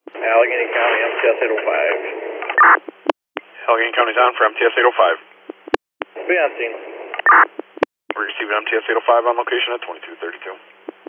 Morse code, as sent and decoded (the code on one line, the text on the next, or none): none